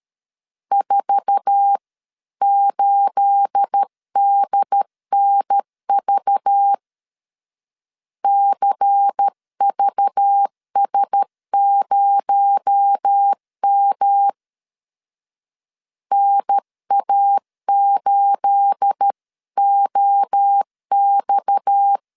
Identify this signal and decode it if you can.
Morse code, heard as a keyed tone, about 13 wpm, 800 Hz. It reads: 48DNV CVS0M NA8OX
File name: LEZIONI / 9 28 12wpm 2351